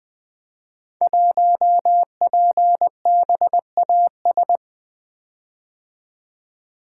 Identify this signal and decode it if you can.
Morse code, heard as a keyed tone, about 20 wpm, 700 Hz. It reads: 1PBAS
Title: Code Groups - 5 Letter:Numbers - 20wpm